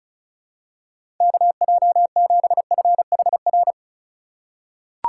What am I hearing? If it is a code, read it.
Morse code, heard as a keyed tone, about 36 wpm, 700 Hz. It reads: KJ7FHR
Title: USA Call Signs - 6 Characters - 2x Character Spacing - 35wpm